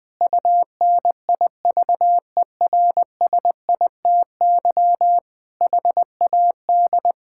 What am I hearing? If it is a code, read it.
Morse code, heard as a keyed tone, about 21 wpm, 700 Hz. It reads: UNIVERSITY HAD
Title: Sentences from Top 1000 Words - Review Part 5 - 20wpm